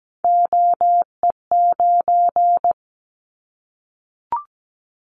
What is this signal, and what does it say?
Morse code, heard as a keyed tone, about 17 wpm, 700 Hz. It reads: OE9
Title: Callsign Prefixes for Top 25 DXCC - Repeated 3x - 17wpm